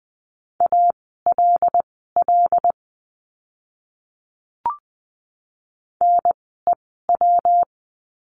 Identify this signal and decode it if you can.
Morse code, heard as a keyed tone, about 20 wpm, spaced as for 14 wpm, 700 Hz. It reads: ALL NEW
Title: 139 Lesson 23f - Letter f - Words 20wpm